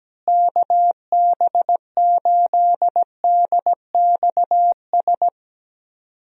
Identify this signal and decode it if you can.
Morse code, heard as a keyed tone, about 17 wpm, 700 Hz. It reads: KB8DXS
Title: USA Call Signs - 6 Characters - Repeated 2x - 17wpm